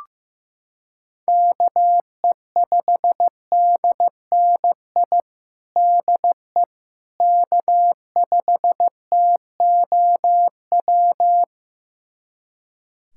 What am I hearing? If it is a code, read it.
Morse code, heard as a keyed tone, about 15 wpm, 700 Hz. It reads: KE5DNI DE K5TOW
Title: USA Callsign DE Callsign 15wpm